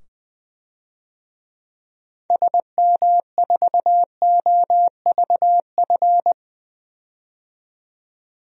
Morse code, heard as a keyed tone, about 20 wpm, 700 Hz. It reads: SM4OVF
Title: DXCC-100 Call Signs - 6 Characters - 20wpm